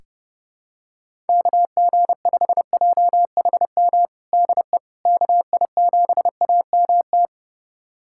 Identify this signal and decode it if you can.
Morse code, heard as a keyed tone, about 30 wpm, 700 Hz. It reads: KG5JHM DE KI7AMT
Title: USA Callsign DE Callsign - 6 Characters - 30wpm